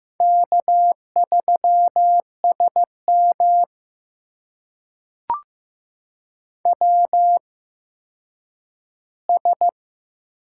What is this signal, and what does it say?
Morse code, heard as a keyed tone, about 15 wpm, spaced as for 9 wpm, 700 Hz. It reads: K3SM WS
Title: USA Call Signs - 4 Characters - 8x Character Spacing - 15wpm